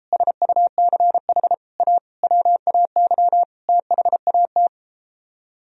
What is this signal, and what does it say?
Morse code, heard as a keyed tone, about 33 wpm, 700 Hz. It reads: SUCH A WAY THAT